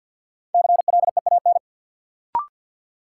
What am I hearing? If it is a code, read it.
Morse code, heard as a keyed tone, about 52 wpm, 700 Hz. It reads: CLEAN
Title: Sentences from Top 1100 Words - New Word Drill Alt - 50wpm